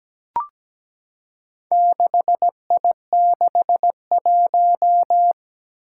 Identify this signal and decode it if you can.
Morse code, heard as a keyed tone, about 17 wpm, 700 Hz. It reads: 6I61
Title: Four Challenging Letter-Number Combinations 17wpm